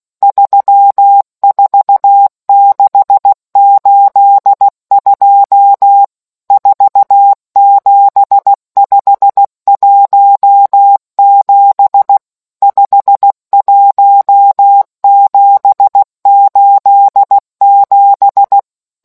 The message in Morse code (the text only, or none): 34682 47517 51787